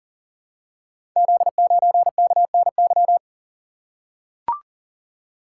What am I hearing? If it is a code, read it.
Morse code, heard as a keyed tone, about 39 wpm, 700 Hz. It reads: Z9KNY
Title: Code Groups - 5 Letter:Numbers - 40wpm